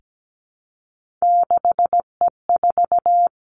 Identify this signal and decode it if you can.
Morse code, heard as a keyed tone, about 17 wpm, 700 Hz. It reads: 6E4
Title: Three Challenging Letter-Number Combinations 17wpm